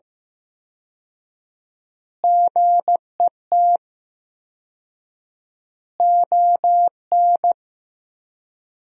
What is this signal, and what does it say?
Morse code, heard as a keyed tone, about 15 wpm, 700 Hz. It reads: GET ON
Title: 4x Word-Spacing - Sentences from Top 200 Words - 15wpm